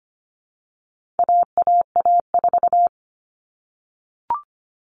Morse code, heard as a keyed tone, about 25 wpm, 700 Hz. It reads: AAA4
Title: All Cut Numbers - 4-Digit 25wpm